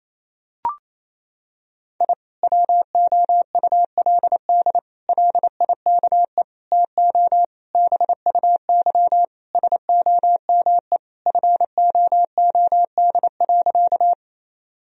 Morse code, heard as a keyed tone, about 28 wpm, 700 Hz. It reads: I WOULD LIKE TO BUY SOME FOOD.
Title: Sentences from Top 1000 Words - Review Part 4 - 28wpm